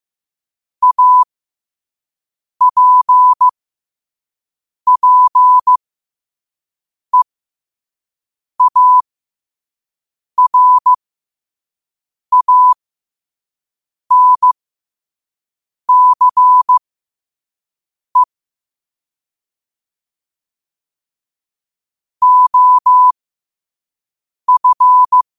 Morse code, heard as a keyed tone, about 15 wpm, spaced as for 5 wpm, 1000 Hz. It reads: APPEARANCE OF
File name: qotdfiles / 5wpm